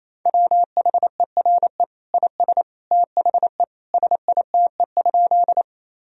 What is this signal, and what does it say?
Morse code, heard as a keyed tone, about 27 wpm, 700 Hz. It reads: WHERE IS THE SITE?
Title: Sentences from Top 1000 Words - Review Part 4 - 28wpm